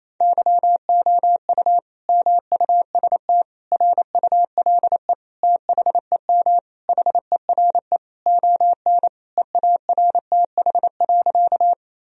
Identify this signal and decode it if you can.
Morse code, heard as a keyed tone, about 28 wpm, 700 Hz. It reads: YOU MUST RULE THEM HERE ON EARTH.
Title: Sentences from Top 1000 Words - Review Part 2 - 28wpm